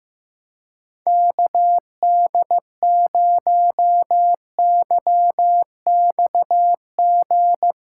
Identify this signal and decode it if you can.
Morse code, heard as a keyed tone, about 15 wpm, 700 Hz. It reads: KD0YXG
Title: Speed Racing - USA 2x3 Call Signs 15wpm